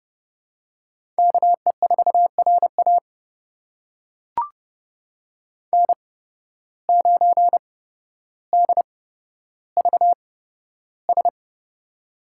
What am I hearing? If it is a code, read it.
Morse code, heard as a keyed tone, about 30 wpm, spaced as for 13 wpm, 700 Hz. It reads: KE4RA N9DVS
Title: USA Call Signs - 5 Characters - 8x Character Spacing - 30wpm